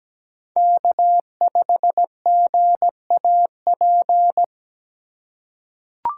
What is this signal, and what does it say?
Morse code, heard as a keyed tone, about 17 wpm, 700 Hz. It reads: K5GAP
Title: USA Call Signs - 5 Characters - 4x Character Spacing - 17wpm